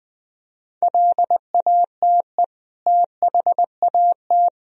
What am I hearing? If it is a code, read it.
Morse code, heard as a keyed tone, about 20 wpm, 700 Hz. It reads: LATE THAT
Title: Sentences from Top 1000 Words - Review Part 4 - 20wpm